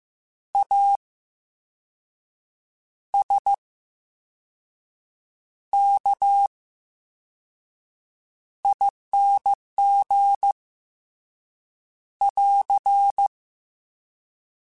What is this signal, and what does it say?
Morse code, heard as a keyed tone, about 15 wpm, 800 Hz. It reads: A S K ING <AR>
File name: Special INGs-mp3s / Special INGs-mp3s / VerySlowToFaster Ing @13